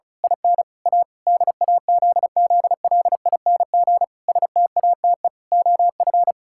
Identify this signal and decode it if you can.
Morse code, heard as a keyed tone, about 36 wpm, 700 Hz. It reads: IN A DAZZLING STATE OF
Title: Pangram Challenge v2 - 35wpm